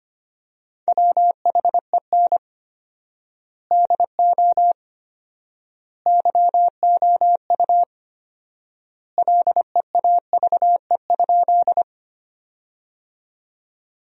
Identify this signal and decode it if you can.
Morse code, heard as a keyed tone, about 25 wpm, 700 Hz. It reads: WHEN DO YOU LEAVE?
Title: 4x Word-Spacing - Sentences from Top 400 Words - 25wpm